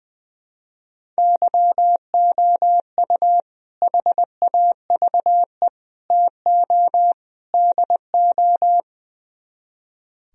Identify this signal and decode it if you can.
Morse code, heard as a keyed tone, about 21 wpm, 700 Hz. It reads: YOU HAVE TO DO
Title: Speed-Racing - Sets of 4 Words 20wpm